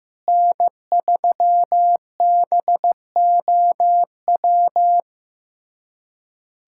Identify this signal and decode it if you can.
Morse code, heard as a keyed tone, about 15 wpm, 700 Hz. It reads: N3BOW
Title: USA Callsign DE Callsign 15wpm